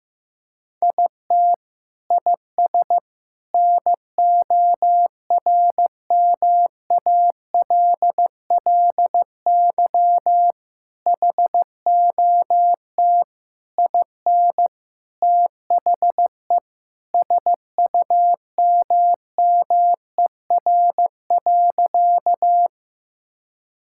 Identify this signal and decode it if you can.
Morse code, heard as a keyed tone, about 15 wpm, 700 Hz. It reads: IT IS NORMALLY HOT IN THE SUMMER.
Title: Sentences from Top 1000 Words - Review Part 6 - 15wpm